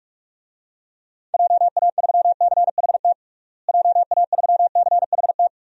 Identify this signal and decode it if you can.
Morse code, heard as a keyed tone, about 41 wpm, 700 Hz. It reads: JA3CHT JA3CHT
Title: DXCC-100 Call Signs - 6 Characters - Repeated 3x - 45wpm